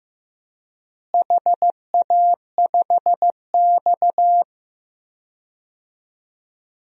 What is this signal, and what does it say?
Morse code, heard as a keyed tone, about 15 wpm, 700 Hz. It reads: HA5X
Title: DXCC-100 Call Signs - 4 Characters - Repeated 2x - 15wpm